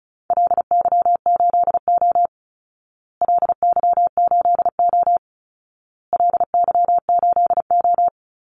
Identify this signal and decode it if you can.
Morse code, heard as a keyed tone, about 34 wpm, 700 Hz. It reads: LY8O LY8O LY8O
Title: Call Signs Repeated 2x - 35wpm